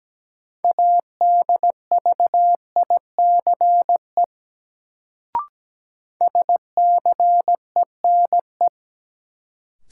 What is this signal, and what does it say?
Morse code, heard as a keyed tone, about 17 wpm, 700 Hz. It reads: ADVICE SCENE
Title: Sentences from Top 1100 Words - New Word Drill Alt - 17wpm